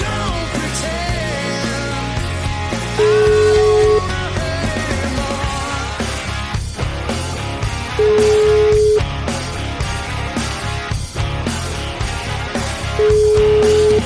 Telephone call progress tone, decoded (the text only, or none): ringback tone